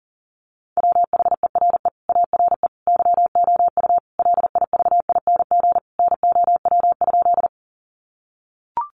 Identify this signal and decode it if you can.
Morse code, heard as a keyed tone, about 40 wpm, 700 Hz. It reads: WHERE ARE YOU LIVING NOW?
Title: Sentences from Top 1100 Words 40wpm